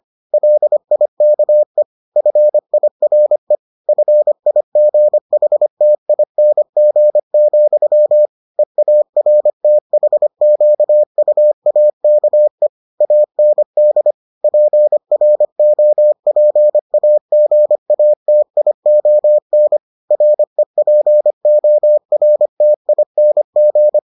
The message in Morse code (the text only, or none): LIKE FIRE FIGHTING, EARTHQUAKE AND PROPAGATION REPORTING